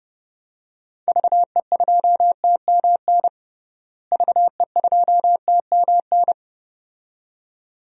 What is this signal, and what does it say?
Morse code, heard as a keyed tone, about 31 wpm, 700 Hz. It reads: VE2TMN VE2TMN
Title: DXCC-100 Call Signs - 6 Characters - Repeated 2x - 30wpm